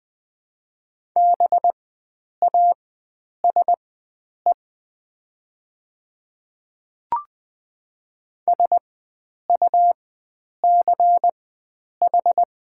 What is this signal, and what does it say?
Morse code, heard as a keyed tone, about 20 wpm, spaced as for 9 wpm, 700 Hz. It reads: BASE SUCH
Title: 4x Character Spacing - Top 500 Words - 20wpm